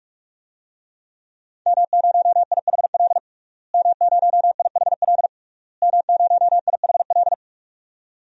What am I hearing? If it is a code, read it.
Morse code, heard as a keyed tone, about 42 wpm, 700 Hz. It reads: M0IHL M0IHL M0IHL
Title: DXCC-100 Call Signs - 5 Characters - Repeated 3x - 45wpm